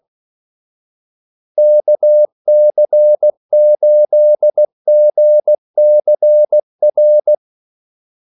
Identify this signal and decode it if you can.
Morse code, heard as a keyed tone, about 16 wpm, 600 Hz. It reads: KC8GCR
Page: Kaldesignaler 16 wpm